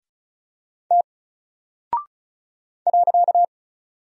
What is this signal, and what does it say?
Morse code, heard as a keyed tone, about 36 wpm, 700 Hz. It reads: T .
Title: Single Character (Letter:Number:Punctuation:Prosign) Alt 35wpm